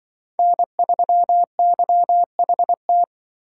Morse code, heard as a keyed tone, about 24 wpm, 700 Hz. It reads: N3YHT
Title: Speed Racing - USA 1x3 Call Signs 20wpm